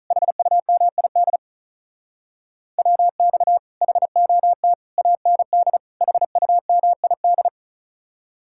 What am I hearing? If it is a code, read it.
Morse code, heard as a keyed tone, about 45 wpm, 700 Hz. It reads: HUMID WX HOT AND HUMID